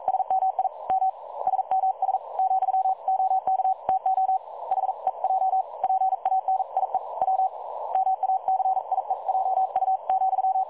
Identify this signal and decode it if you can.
Morse code, heard as a keyed tone, about 42 wpm, 760 Hz. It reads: SMI M SMS , OK TO HEJ PANIE W MACIEJU,
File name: dlazaawansowanych / nagraniazpasma / websdr recording 3565.0kHz